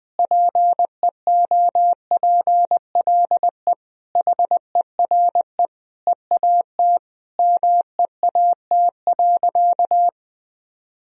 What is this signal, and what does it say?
Morse code, heard as a keyed tone, about 20 wpm, 700 Hz. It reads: PEOPLE HERE EAT MEAT.
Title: Sentences from Top 1000 Words - Review Part 3 - 20wpm